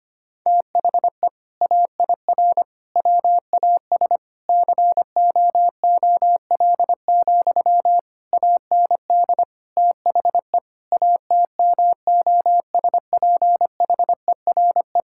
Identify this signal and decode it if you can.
Morse code, heard as a keyed tone, about 25 wpm, 700 Hz. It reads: THE AIR WAS COOL, AND THE ATMOSPHERE
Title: Sentences from Top 1600 Words 25wpm